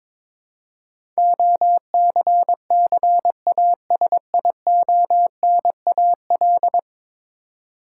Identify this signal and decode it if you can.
Morse code, heard as a keyed tone, about 22 wpm, 700 Hz. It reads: OCCASIONAL
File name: practice / 22wpm / 10 Letter Words - First 3 Letters Given - 22wpm